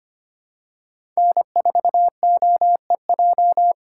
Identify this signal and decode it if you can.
Morse code, heard as a keyed tone, about 25 wpm, 700 Hz. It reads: N4OEJ
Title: USA Call Signs - 5 Characters - 2x Character Spacing - 25wpm